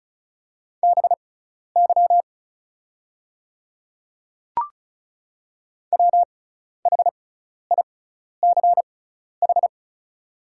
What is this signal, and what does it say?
Morse code, heard as a keyed tone, about 36 wpm, spaced as for 12 wpm, 700 Hz. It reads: BY WHICH
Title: 6x Character Spacing - Top 200 Words - 35wpm